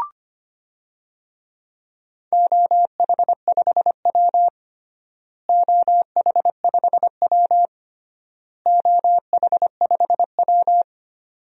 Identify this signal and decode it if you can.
Morse code, heard as a keyed tone, about 25 wpm, 700 Hz. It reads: OH5W OH5W OH5W
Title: DXCC-100 Call Signs - 4 Characters - Repeated 3x - 25wpm